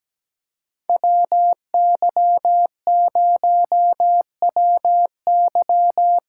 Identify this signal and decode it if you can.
Morse code, heard as a keyed tone, about 17 wpm, 700 Hz. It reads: WY0WY
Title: USA Call Signs - 6 Characters - 8x Character Spacing - 17wpm